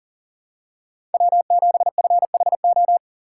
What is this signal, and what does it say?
Morse code, heard as a keyed tone, about 40 wpm, 700 Hz. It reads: W7FHO
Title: USA Call Signs - 5 Characters - 4x Character Spacing - 40wpm